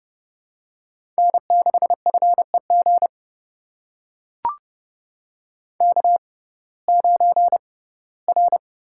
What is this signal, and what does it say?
Morse code, heard as a keyed tone, about 30 wpm, spaced as for 18 wpm, 700 Hz. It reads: N6FEG K9R
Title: USA Call Signs - 5 Characters - 6x Character Spacing - 30wpm